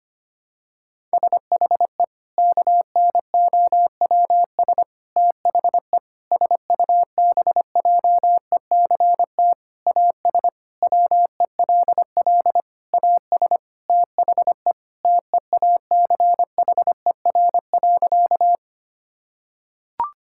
Morse code, heard as a keyed tone, about 25 wpm, 700 Hz. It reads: SHE KNOWS THE SUBJECT AS WELL AS THE TEACHER.
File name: practice / 25wpm / Sentences with 4-Word Phrases - 25wpm